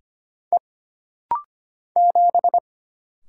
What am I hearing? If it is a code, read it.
Morse code, heard as a keyed tone, about 25 wpm, 700 Hz. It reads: E 7
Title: Single Letter-Number Alt - Rapid-Fire 25wpm